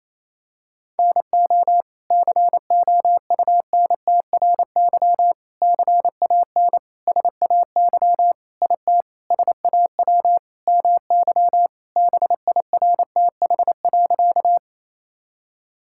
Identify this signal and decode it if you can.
Morse code, heard as a keyed tone, about 28 wpm, 700 Hz. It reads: NO COUNTRY CAN SAY IT SAW MY BIRTH.
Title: Sentences from Top 1000 Words - Review Part 5 - 28wpm